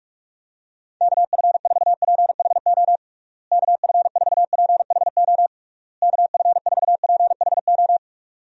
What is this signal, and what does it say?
Morse code, heard as a keyed tone, about 46 wpm, 700 Hz. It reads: KF4PHO KF4PHO KF4PHO
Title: USA Call Signs - 6 Characters - Repeated 3x - 45wpm